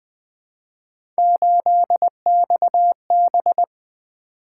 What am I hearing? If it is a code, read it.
Morse code, heard as a keyed tone, about 20 wpm, 700 Hz. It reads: 8XB
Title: Three Letter-Number Combinations - 20wpm